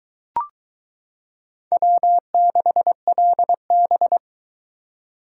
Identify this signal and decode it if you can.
Morse code, heard as a keyed tone, about 22 wpm, 700 Hz. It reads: W6LB